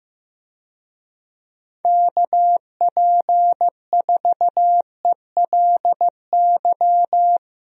Morse code, heard as a keyed tone, about 16 wpm, 700 Hz. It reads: KP4ELY
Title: USA Call Signs - 6 Characters - Repeated 3x - 15wpm